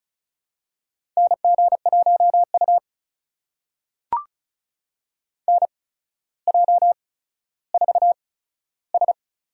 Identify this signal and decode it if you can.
Morse code, heard as a keyed tone, about 36 wpm, spaced as for 15 wpm, 700 Hz. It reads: NG1U NJ4S
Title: USA Call Signs - 4 Characters - 8x Character Spacing - 35wpm